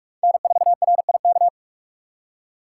Morse code, heard as a keyed tone, about 41 wpm, 700 Hz. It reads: N4RIK
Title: USA Call Signs - 5 Characters - 4x Character Spacing - 45wpm